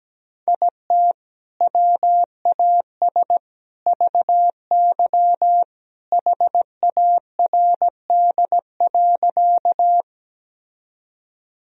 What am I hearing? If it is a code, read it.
Morse code, heard as a keyed tone, about 17 wpm, 700 Hz. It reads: IT WAS VY HARD.
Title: Sentences from Top 300 Words and Abbreviations 17wpm